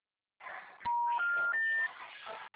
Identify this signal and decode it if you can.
Special information tone, with noise.
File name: out- -2006- - - .703